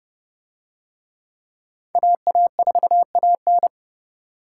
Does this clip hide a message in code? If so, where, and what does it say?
Yes: Morse code from 2.0 to 3.7 s, heard as a keyed tone, about 29 wpm, 700 Hz. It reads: AA4AN